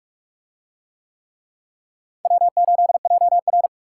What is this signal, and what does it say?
Morse code, heard as a keyed tone, about 40 wpm, 700 Hz. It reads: W8JR